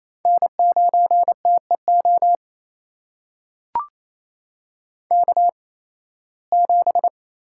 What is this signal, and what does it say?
Morse code, heard as a keyed tone, about 28 wpm, spaced as for 18 wpm, 700 Hz. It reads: N9TEO K7
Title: USA Call Signs - 5 Characters - 8x Character Spacing - 28wpm